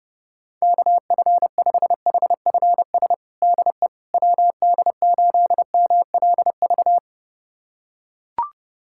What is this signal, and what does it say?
Morse code, heard as a keyed tone, about 32 wpm, 700 Hz. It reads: KF5HFS DE WD8MLV
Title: USA Callsign DE Callsign - 6 Characters - 30wpm